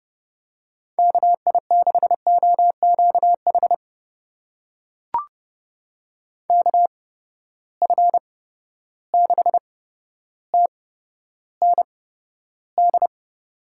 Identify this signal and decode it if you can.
Morse code, heard as a keyed tone, about 31 wpm, spaced as for 13 wpm, 700 Hz. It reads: KI6OQH KF6TND
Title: USA Call Signs - 6 Characters - 8x Character Spacing - 30wpm